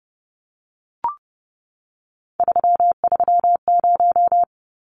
Morse code, heard as a keyed tone, about 30 wpm, 700 Hz. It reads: 330
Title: Numbers - 3-Digit 30wpm